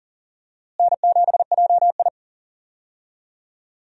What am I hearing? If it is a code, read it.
Morse code, heard as a keyed tone, about 40 wpm, 700 Hz. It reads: N7JI
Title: USA Call Signs - 4 Characters - Repeated 3x - 40wpm